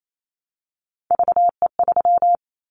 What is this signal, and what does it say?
Morse code, heard as a keyed tone, about 28 wpm, 700 Hz. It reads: VE3
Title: Callsign Prefixes for Top 25 DXCC - Repeated 3x - 28wpm